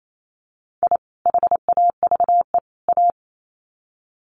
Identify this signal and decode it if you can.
Morse code, heard as a keyed tone, about 28 wpm, 700 Hz. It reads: I HAVE A
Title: Sets of 3 Words 28wpm